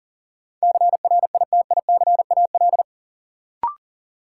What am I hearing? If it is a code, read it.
Morse code, heard as a keyed tone, about 40 wpm, 700 Hz. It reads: CRITICAL
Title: Sentences from Top 1100 Words - New Word Drill - 40wpm